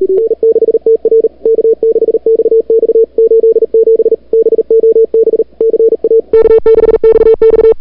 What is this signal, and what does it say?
Morse code, heard as a keyed tone, about 40 wpm, 440 Hz. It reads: N6TR K6XX 87 BOB CA K6XX
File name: n6tr / Fab2007CWSprintHour1